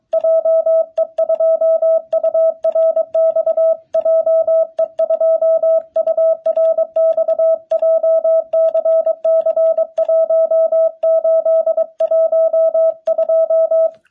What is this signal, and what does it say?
Morse code, heard as a keyed tone, about 24 wpm, 660 Hz. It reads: JE2URX JE2URX JCC 1812
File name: jh8cbh / 2021cw